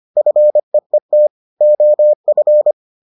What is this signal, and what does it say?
Morse code, heard as a keyed tone, about 25 wpm, 600 Hz. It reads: FEET OF